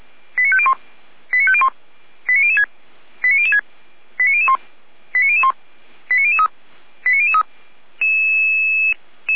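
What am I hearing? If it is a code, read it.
Five-tone selective call (ZVEI), heard as quick runs of five tones, tones of about 70 ms. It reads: 89501 89501 89006 89006 89001 89001 89003 89003
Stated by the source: ZVEI five-tone call received by SDR.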